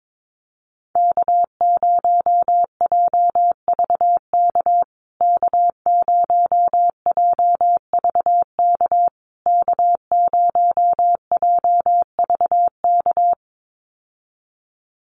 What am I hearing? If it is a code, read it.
Morse code, heard as a keyed tone, about 22 wpm, 690 Hz. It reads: K0JVK K0JVK K0JVK
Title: USA Call Signs Repeated 3x 22wpm